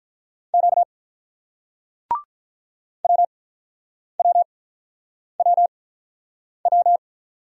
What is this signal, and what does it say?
Morse code, heard as a keyed tone, about 49 wpm, spaced as for 9 wpm, 700 Hz. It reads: Q WWWW